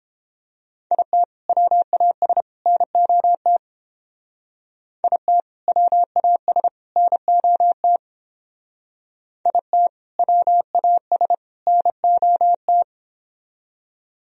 Morse code, heard as a keyed tone, about 28 wpm, 700 Hz. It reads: IT WAS NOT IT WAS NOT IT WAS NOT